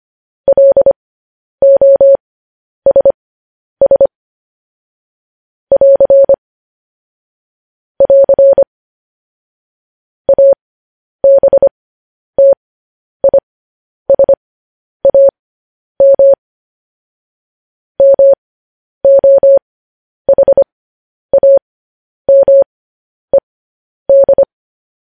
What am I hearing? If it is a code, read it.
Morse code, heard as a keyed tone, about 25 wpm, spaced as for 10 wpm, 550 Hz. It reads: LOSS <AR> <AR> ABTISAM MOHAMED
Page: news headlines in morse code at 10 wpm